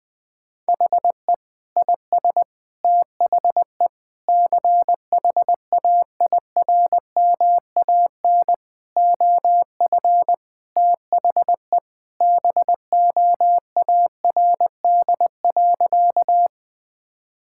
Morse code, heard as a keyed tone, about 20 wpm, 700 Hz. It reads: HE IS THE CHAIRMAN OF THE BOARD.
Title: Sentences with 3-Word Phrases - Spoken Beginning - 20wpm